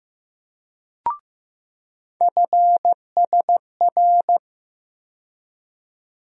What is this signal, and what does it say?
Morse code, heard as a keyed tone, about 15 wpm, 700 Hz. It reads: FSR
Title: USA Call Sign Postfixes - 3 Letters 15wpm